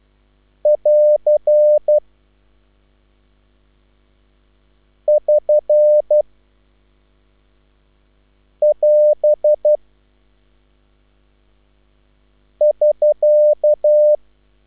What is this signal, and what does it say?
Morse code, heard as a keyed tone, about 12 wpm, 600 Hz. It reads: <AR> <SN> <AS> <SK>
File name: QSL / CW lesson